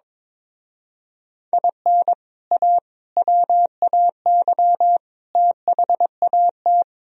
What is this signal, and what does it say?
Morse code, heard as a keyed tone, about 22 wpm, 700 Hz. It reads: IN A WAY THAT
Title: Speed-Racing - Sets of 4 Words 22wpm